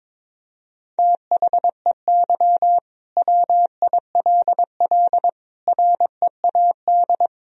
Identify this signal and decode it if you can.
Morse code, heard as a keyed tone, about 22 wpm, 700 Hz. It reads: THEY WILL READ
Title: Sentences from Top 1000 Words - Review Part 4 - 22wpm